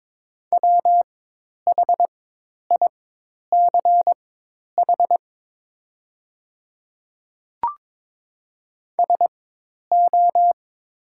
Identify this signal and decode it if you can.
Morse code, heard as a keyed tone, about 22 wpm, spaced as for 10 wpm, 700 Hz. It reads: WHICH SO
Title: 4x Character Spacing - Top 100 Words - 22wpm